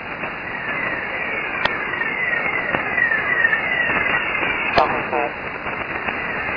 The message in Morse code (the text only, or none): none